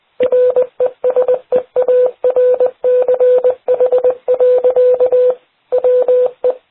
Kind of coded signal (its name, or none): Morse code